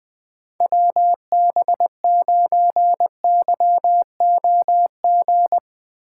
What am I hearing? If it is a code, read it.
Morse code, heard as a keyed tone, about 20 wpm, 700 Hz. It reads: WB9YOG
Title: USA Call Signs - 6 Characters - 2x Character Spacing - 20wpm